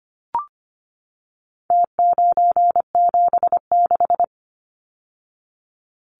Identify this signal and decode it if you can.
Morse code, heard as a keyed tone, about 25 wpm, 700 Hz. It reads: T976
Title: Numbers - Contest Style - 4-Digit 25wpm